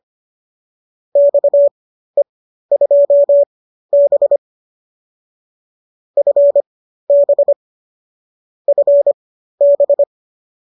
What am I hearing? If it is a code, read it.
Morse code, heard as a keyed tone, about 25 wpm, spaced as for 13 wpm, 570 Hz. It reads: XE2B FB FB